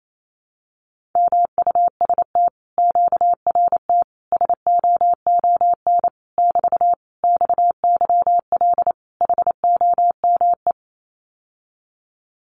Morse code, heard as a keyed tone, about 28 wpm, 700 Hz. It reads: MUST QRT SOON <BT> XYL HOME
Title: Common QSO Phrases Encore 28wpm